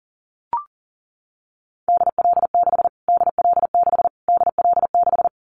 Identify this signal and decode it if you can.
Morse code, heard as a keyed tone, about 40 wpm, 700 Hz. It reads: DL6 DL6 DL6
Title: Callsign Prefixes for Top 25 DXCC - Repeated 3x - 40wpm